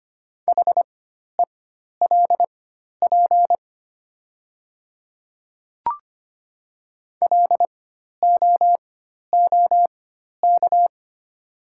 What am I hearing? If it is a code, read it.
Morse code, heard as a keyed tone, about 25 wpm, spaced as for 12 wpm, 700 Hz. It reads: HELP LOOK
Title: 4x Character Spacing - Top 300 Words - 25wpm